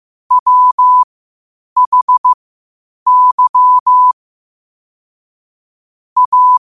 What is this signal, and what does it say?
Morse code, heard as a keyed tone, about 15 wpm, spaced as for 8 wpm, 1000 Hz. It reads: WHY A
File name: qotdfiles / 8wpm